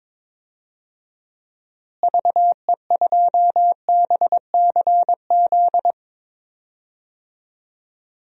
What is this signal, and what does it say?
Morse code, heard as a keyed tone, about 21 wpm, 700 Hz. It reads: VE2BCZ
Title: DXCC-100 Call Signs - 6 Characters - Repeated 2x - 22wpm